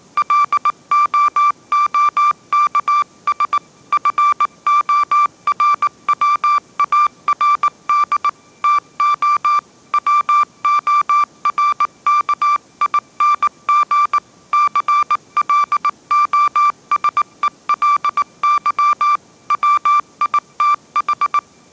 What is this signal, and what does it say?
Morse code, heard as a keyed tone, about 19 wpm, 1180 Hz. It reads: LOOKS FORWARD TO WORKING CLOSELY WITH
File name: mrx083 20wpm